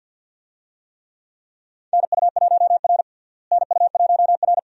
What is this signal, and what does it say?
Morse code, heard as a keyed tone, about 52 wpm, 700 Hz. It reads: NU1R NU1R
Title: USA Call Signs - 4 Characters - Repeated 2x - 50wpm